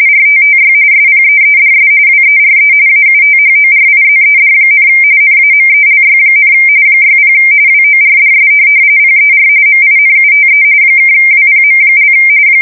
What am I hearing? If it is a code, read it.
RTTY, about 45.45 baud. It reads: RTTY SPEEDS WERE AUTHORIZED, NAMELY, 60 WORDS PER MINUTE (WPM)
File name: fsk170 / teletype-RTTY